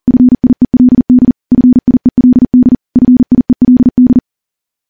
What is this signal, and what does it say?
Morse code, heard as a keyed tone, about 40 wpm, 250 Hz. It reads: FIELD FIELD FIELD
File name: morse / morse-training-40WPM